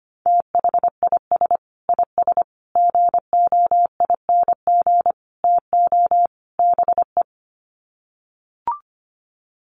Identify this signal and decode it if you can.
Morse code, heard as a keyed tone, about 25 wpm, 700 Hz. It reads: THIS IS GOING TO BE
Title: Sets of 5 Words 25wpm